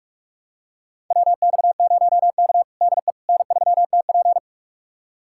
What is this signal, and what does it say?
Morse code, heard as a keyed tone, about 46 wpm, 700 Hz. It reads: WX0K DE N3TP
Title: USA Callsign DE Callsign - 4 Characters - 45wpm